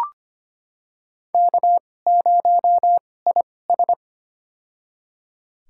Morse code, heard as a keyed tone, about 25 wpm, spaced as for 18 wpm, 700 Hz. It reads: K0IS
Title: USA Call Signs - 4 Characters - 2x Character Spacing - 25wpm